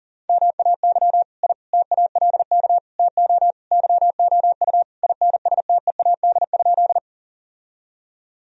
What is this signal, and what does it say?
Morse code, heard as a keyed tone, about 40 wpm, 700 Hz. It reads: MAY I TALK TO YOU INSTEAD?
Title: Sentences from Top 1000 Words - Review Part 4 - 40wpm